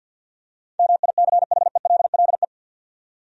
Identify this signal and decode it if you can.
Morse code, heard as a keyed tone, about 47 wpm, 700 Hz. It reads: MICHELLE
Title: USA Common Names - 50wpm